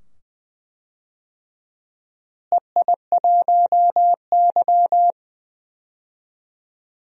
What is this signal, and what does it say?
Morse code, heard as a keyed tone, about 21 wpm, 700 Hz. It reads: EI1Y
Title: DXCC-100 Call Signs - 4 Characters - Repeated 2x - 20wpm